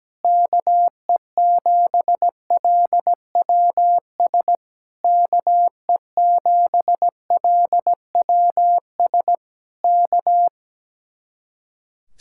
Morse code, heard as a keyed tone, about 17 wpm, 700 Hz. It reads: KE7LWS KE7LWS K
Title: Speed-Racing Common QSO Phrases - 17wpm